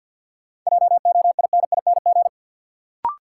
Morse code, heard as a keyed tone, about 53 wpm, 700 Hz. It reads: JOINING
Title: Sets of 5 Words - New Word Drill - 50wpm